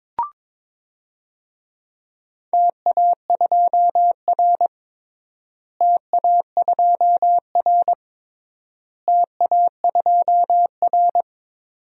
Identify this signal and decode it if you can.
Morse code, heard as a keyed tone, about 22 wpm, 700 Hz. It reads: TA2R TA2R TA2R
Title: DXCC-100 Call Signs - 4 Characters - Repeated 3x - 22wpm